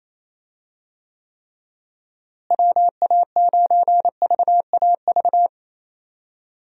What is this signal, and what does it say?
Morse code, heard as a keyed tone, about 27 wpm, 700 Hz. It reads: WA9VAV
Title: USA Call Signs - 6 Characters - Repeated 3x - 28wpm